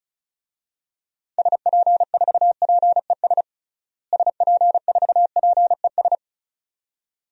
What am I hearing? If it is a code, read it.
Morse code, heard as a keyed tone, about 34 wpm, 700 Hz. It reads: SP4PES SP4PES
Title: DXCC-100 Call Signs - 6 Characters - Repeated 2x - 35wpm